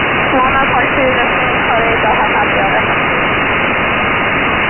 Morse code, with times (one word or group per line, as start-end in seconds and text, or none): none